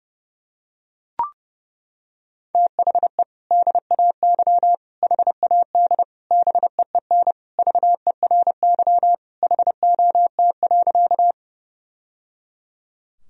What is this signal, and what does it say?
Morse code, heard as a keyed tone, about 30 wpm, 700 Hz. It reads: THE DAY HAD BEEN VERY HOT.
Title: Sentences from Top 1000 Words - Review Part 3 - 30wpm